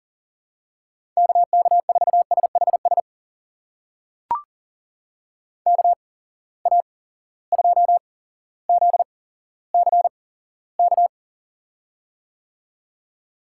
Morse code, heard as a keyed tone, about 40 wpm, spaced as for 17 wpm, 700 Hz. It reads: KK4SHS KA2ZCK
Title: USA Call Signs - 6 Characters - 8x Character Spacing - 40wpm